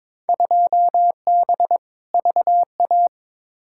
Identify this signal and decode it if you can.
Morse code, heard as a keyed tone, about 22 wpm, 700 Hz. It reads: 2B VA
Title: Field Day Exchanges Encore - 22wpm